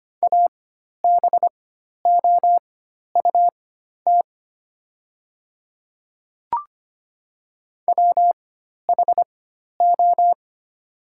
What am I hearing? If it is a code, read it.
Morse code, heard as a keyed tone, about 25 wpm, spaced as for 12 wpm, 700 Hz. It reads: ABOUT WHO
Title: 4x Character Spacing - Top 100 Words - 25wpm